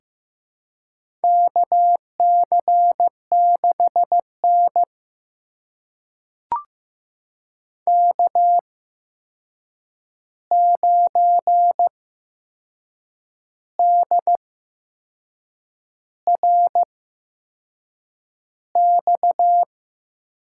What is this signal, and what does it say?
Morse code, heard as a keyed tone, about 15 wpm, spaced as for 6 wpm, 700 Hz. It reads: KC6N K9DRX
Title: USA Call Signs - 8x Character Spacing - 15wpm